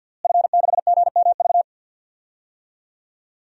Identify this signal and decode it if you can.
Morse code, heard as a keyed tone, about 51 wpm, 700 Hz. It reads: F6GMV
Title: DXCC-100 Call Signs - 5 Characters - Repeated 3x - 50wpm